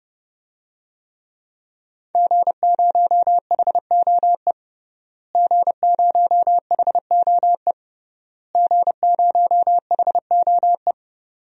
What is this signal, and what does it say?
Morse code, heard as a keyed tone, about 32 wpm, 700 Hz. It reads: G0HOE G0HOE G0HOE
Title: DXCC-100 Call Signs - 5 Characters - Repeated 3x - 30wpm